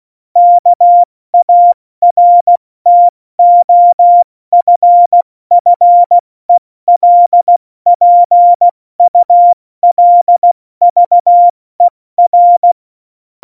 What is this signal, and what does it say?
Morse code, heard as a keyed tone, about 16 wpm, 700 Hz. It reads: KARTOFFELPULVER
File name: Morsekurs / Witze / Witz 02. Januar 2018, 70 F80